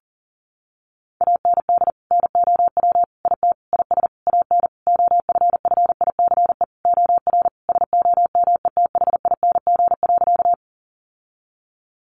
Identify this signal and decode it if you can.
Morse code, heard as a keyed tone, about 40 wpm, 700 Hz. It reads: AND NOW IT IS AN OFFICE OR SOMETHING.
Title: Sentences from Top 1000 Words 40wpm